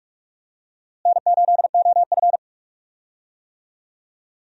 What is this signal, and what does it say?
Morse code, heard as a keyed tone, about 46 wpm, 700 Hz. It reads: N8OF
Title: USA Call Signs - 4 Characters - Repeated 3x - 45wpm